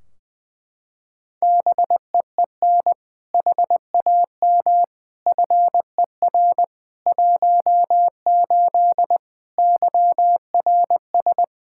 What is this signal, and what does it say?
Morse code, heard as a keyed tone, about 20 wpm, 700 Hz. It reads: BEEN HAM FER 18 YRS
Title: Practice QSO - Exchange 3 - More Info - 20wpm